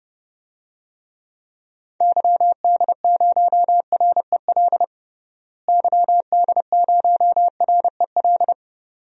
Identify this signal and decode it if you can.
Morse code, heard as a keyed tone, about 32 wpm, 700 Hz. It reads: YD0REL YD0REL
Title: DXCC-100 Call Signs - 6 Characters - Repeated 2x - 30wpm